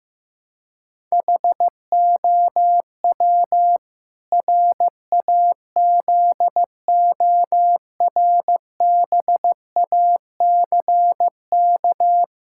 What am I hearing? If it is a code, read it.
Morse code, heard as a keyed tone, about 15 wpm, 700 Hz. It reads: HOW RAZORBACK
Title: Pangram Challenge v2 - 15wpm